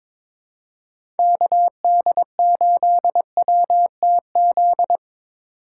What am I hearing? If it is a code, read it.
Morse code, heard as a keyed tone, about 22 wpm, 700 Hz. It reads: KD8WTZ
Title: USA Call Signs - 6 Characters - 3x Character Spacing - 22wpm